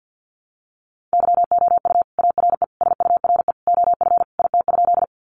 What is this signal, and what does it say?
Morse code, heard as a keyed tone, about 50 wpm, 700 Hz. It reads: YOU ARE SURE OF IT?
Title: Sentences from Top 300 Words 50wpm